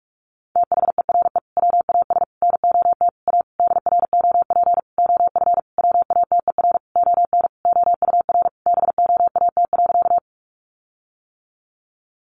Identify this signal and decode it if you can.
Morse code, heard as a keyed tone, about 44 wpm, 700 Hz. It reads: THERE WAS NOT A DROP OF WATER ON OUR BOAT.
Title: Sentences from Top 1400 Words 45wpm